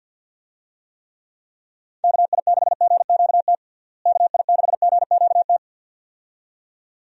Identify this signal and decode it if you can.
Morse code, heard as a keyed tone, about 48 wpm, 700 Hz. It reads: KI6GQT KI6GQT
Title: USA Call Signs - 6 Characters - Repeated 2x - 50wpm